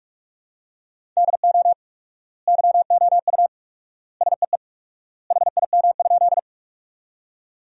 Morse code, heard as a keyed tone, about 46 wpm, 700 Hz. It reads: DO YOU SEE HIM?
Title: 4x Word-Spacing - Sentences from Top 100 Words - 45wpm